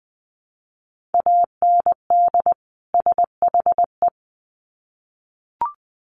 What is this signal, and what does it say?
Morse code, heard as a keyed tone, about 20 wpm, 700 Hz. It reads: AND SHE
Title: Sets of 2 Words 20wpm